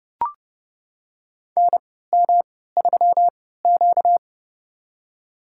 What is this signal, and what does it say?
Morse code, heard as a keyed tone, about 31 wpm, spaced as for 17 wpm, 700 Hz. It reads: NM3Q
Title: USA Call Signs - 4 Characters - 3x Character Spacing - 30wpm